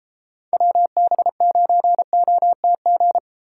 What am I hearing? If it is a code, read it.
Morse code, heard as a keyed tone, about 33 wpm, 700 Hz. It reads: WB9OTG